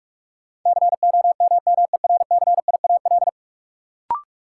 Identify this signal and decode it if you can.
Morse code, heard as a keyed tone, about 41 wpm, 700 Hz. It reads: COMMERCIAL
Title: Sentences from Top 1000 Words - Review Part 5 - New Word Drill - 45wpm